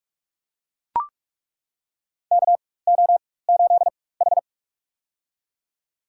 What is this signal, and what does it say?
Morse code, heard as a keyed tone, about 41 wpm, spaced as for 20 wpm, 700 Hz. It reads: KO8H
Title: USA Call Signs - 4 Characters - 4x Character Spacing - 45wpm